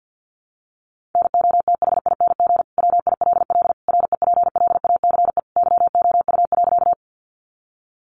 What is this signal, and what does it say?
Morse code, heard as a keyed tone, about 48 wpm, 700 Hz. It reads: NOTHING WILL REPLACE YOU.
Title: Sentences from Top 1400 Words 50wpm